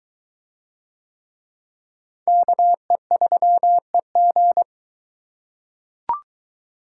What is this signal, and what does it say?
Morse code, heard as a keyed tone, about 22 wpm, 700 Hz. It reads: KE3EG